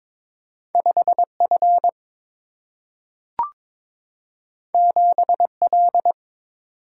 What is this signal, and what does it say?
Morse code, heard as a keyed tone, about 22 wpm, 700 Hz. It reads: 5 F 7 L
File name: practice / 22wpm / Two Letter-Number Combinations - 22wpm